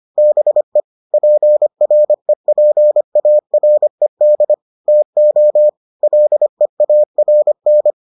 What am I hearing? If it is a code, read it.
Morse code, heard as a keyed tone, about 25 wpm, 600 Hz. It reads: BE PREPARED TO LEARN